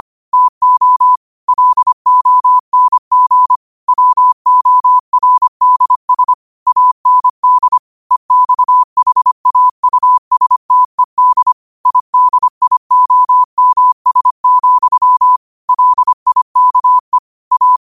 Morse code, heard as a keyed tone, about 25 wpm, 1000 Hz. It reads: TO LONG WORDS AND EXHAUSTED IDIOMS, LIKE A
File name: 25wpm